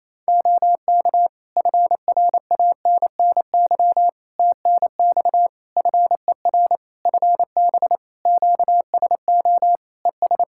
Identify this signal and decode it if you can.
Morse code, heard as a keyed tone, about 28 wpm, 700 Hz. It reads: OK FRANNY TNX FER FB QSO ES
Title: Practice QSO - Exchange 4 - Ending - 28wpm